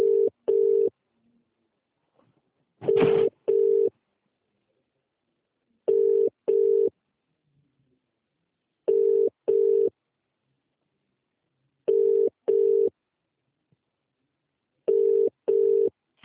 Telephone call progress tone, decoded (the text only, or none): ringback tone